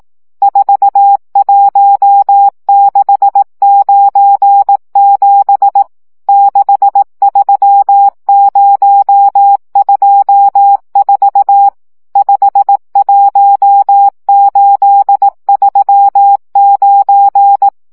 Morse code, heard as a keyed tone, about 18 wpm, 800 Hz. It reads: 41697 63024 51839
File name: morze vizsga 2011-3 15